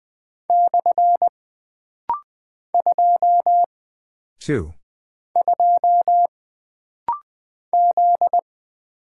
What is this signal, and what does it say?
Morse code, heard as a keyed tone, about 20 wpm, 700 Hz. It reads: / 2 2 Z
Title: Single Character (Letter:Number:Punctuation:Prosign) - Rapid-Fire - 20wpm